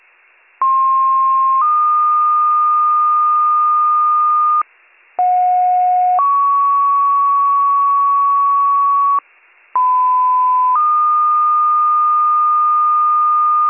Two-tone page, heard as a one-second tone, then a three-second tone, about 1060 then 1220 Hz, about 730 then 1120 Hz, about 1010 then 1220 Hz. It is heard from 0.6 to 13.7 s.